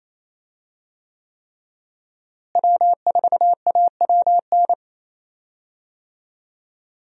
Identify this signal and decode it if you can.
Morse code, heard as a keyed tone, about 27 wpm, 700 Hz. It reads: W4AWN
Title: USA Call Signs - 5 Characters - Repeated 3x - 28wpm